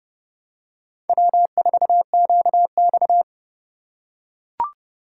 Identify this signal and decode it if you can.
Morse code, heard as a keyed tone, about 29 wpm, 700 Hz. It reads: W4QX